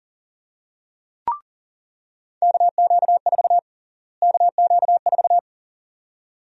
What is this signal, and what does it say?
Morse code, heard as a keyed tone, about 40 wpm, 700 Hz. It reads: KQ4 KQ4
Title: USA Call Sign Prefixes - Repeated 2x - 40wpm